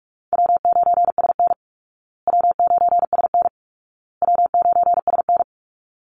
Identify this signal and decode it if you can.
Morse code, heard as a keyed tone, about 44 wpm, 700 Hz. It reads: W9SN W9SN W9SN
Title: Easy Call Signs Repeated 3x - 45wpm